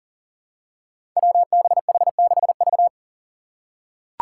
Morse code, heard as a keyed tone, about 39 wpm, 700 Hz. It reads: WBH6V
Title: Five Challenging Letter-Number Combinations 40wpm